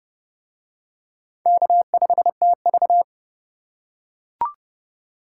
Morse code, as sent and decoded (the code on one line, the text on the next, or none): -.- ..... - ...-
K5TV